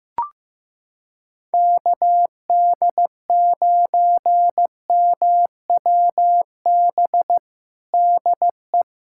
Morse code, heard as a keyed tone, about 15 wpm, 700 Hz. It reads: KD9MWB DE
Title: USA Callsign DE Callsign - 6 Characters - 15wpm